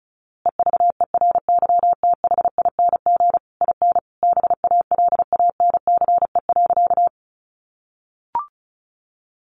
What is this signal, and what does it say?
Morse code, heard as a keyed tone, about 34 wpm, 700 Hz. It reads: EVERYTHING IN BALANCE.
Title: A Sound Of Thunder 35wpm